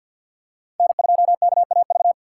Morse code, heard as a keyed tone, about 53 wpm, 700 Hz. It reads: N2KAV
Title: USA Call Signs - 5 Characters - 4x Character Spacing - 50wpm